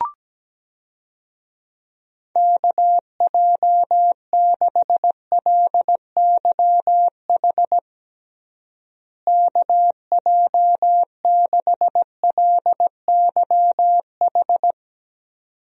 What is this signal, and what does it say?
Morse code, heard as a keyed tone, about 17 wpm, 700 Hz. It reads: KJ6LYH KJ6LYH
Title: USA Call Signs - 6 Characters - Repeated 2x - 17wpm